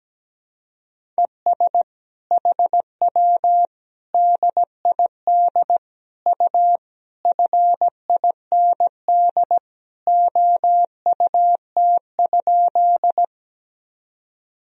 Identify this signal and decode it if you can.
Morse code, heard as a keyed tone, about 17 wpm, 700 Hz. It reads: ES HW DID U FIND OUT?
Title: Sentences from Top 100 Words and Abbreviations 17wpm